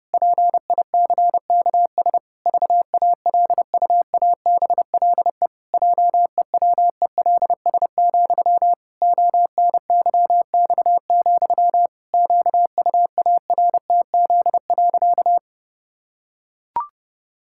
Morse code, heard as a keyed tone, about 30 wpm, 700 Hz. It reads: PICKS VALUABLE JEWELS, ONYX, QUARTZ.
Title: Pangram Challenge v2 - 30wpm